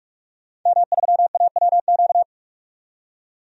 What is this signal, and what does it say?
Morse code, heard as a keyed tone, about 46 wpm, 700 Hz. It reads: M3AWQ
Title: DXCC-100 Call Signs - 5 Characters - 45wpm